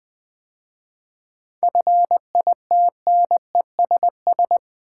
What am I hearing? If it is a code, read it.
Morse code, heard as a keyed tone, about 20 wpm, 700 Hz. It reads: FITNESS
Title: 7 Letter Words - First 3 Letters Given - 20wpm